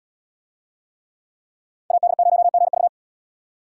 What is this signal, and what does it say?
Morse code, heard as a keyed tone, about 75 wpm, 700 Hz. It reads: AD0GV